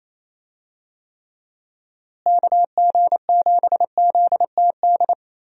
Morse code, heard as a keyed tone, about 28 wpm, 700 Hz. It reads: KG7ZTD
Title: USA Call Signs - 6 Characters - Repeated 3x - 28wpm